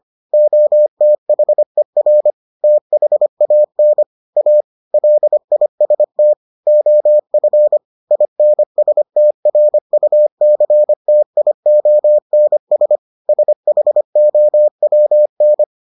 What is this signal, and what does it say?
Morse code, heard as a keyed tone, about 25 wpm, 600 Hz. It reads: OTHER THAN A LIST OF INSTRUCTIONS SHOWN